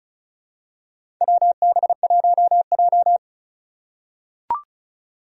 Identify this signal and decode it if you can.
Morse code, heard as a keyed tone, about 36 wpm, 700 Hz. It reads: WB1J
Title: USA Call Signs - 4 Characters - 8x Character Spacing - 35wpm